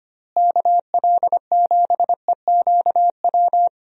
Morse code, heard as a keyed tone, about 25 wpm, 700 Hz. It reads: KL7EQW
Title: DXCC-100 Call Signs - 6 Characters - Repeated 3x - 25wpm